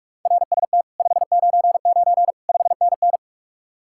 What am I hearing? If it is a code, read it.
Morse code, heard as a keyed tone, about 58 wpm, 700 Hz. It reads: RST 599 5NN